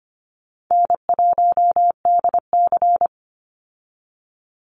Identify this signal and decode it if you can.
Morse code, heard as a keyed tone, about 25 wpm, 700 Hz. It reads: N1DC
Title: Call Signs - 25wpm